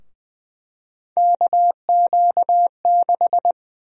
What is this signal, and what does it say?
Morse code, heard as a keyed tone, about 20 wpm, 700 Hz. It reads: KQ6
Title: USA Call Sign Prefixes - Repeated 2x - 20wpm